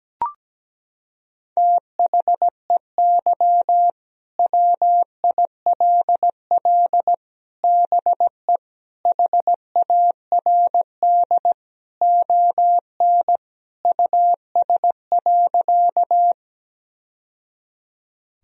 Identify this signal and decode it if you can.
Morse code, heard as a keyed tone, about 17 wpm, 700 Hz. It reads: THEY WILL BE HARD ON US.
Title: Sentences from Top 1000 Words - Review Part 1 - 17wpm